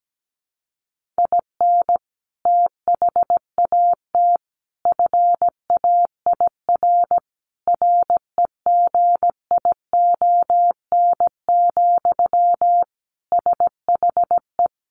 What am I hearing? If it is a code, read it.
Morse code, heard as a keyed tone, about 17 wpm, 700 Hz. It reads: IN THAT FAIR REGION, SHE
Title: Sentences from Top 1500 Words 17wpm